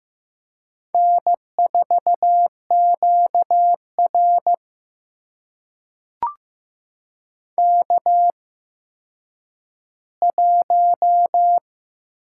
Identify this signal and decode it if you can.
Morse code, heard as a keyed tone, about 15 wpm, spaced as for 9 wpm, 700 Hz. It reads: N4QR K1
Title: USA Call Signs - 4 Characters - 8x Character Spacing - 15wpm